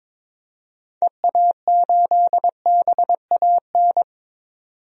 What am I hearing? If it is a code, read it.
Morse code, heard as a keyed tone, about 22 wpm, 700 Hz. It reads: EA8BAN
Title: DXCC-100 Call Signs - 6 Characters - 22wpm